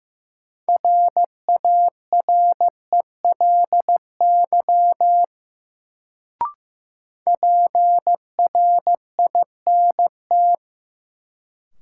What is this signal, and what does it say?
Morse code, heard as a keyed tone, about 15 wpm, 700 Hz. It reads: RARELY PRINT
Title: Sentences from Top 1400 Words - New Word Drill - 15wpm